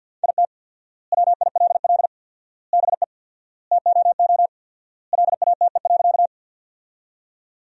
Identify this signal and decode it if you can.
Morse code, heard as a keyed tone, about 53 wpm, 700 Hz. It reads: IT WILL BE TOO LATE.
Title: 4x Word-Spacing - Sentences from Top 300 Words - 50wpm